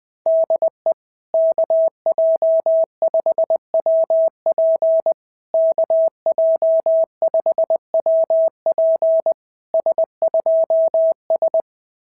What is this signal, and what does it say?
Morse code, heard as a keyed tone, about 19 wpm, 650 Hz. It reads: DE KJ5WP KJ5WP S2S
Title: SOTA QSO Practice 20wpm